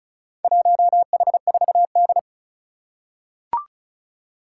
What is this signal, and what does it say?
Morse code, heard as a keyed tone, about 36 wpm, 700 Hz. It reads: 1H4D
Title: Four Challenging Letter-Number Combinations 35wpm